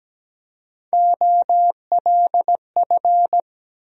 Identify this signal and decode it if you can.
Morse code, heard as a keyed tone, about 17 wpm, 700 Hz. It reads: OLF
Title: USA Call Sign Postfixes - 3 Letters 17wpm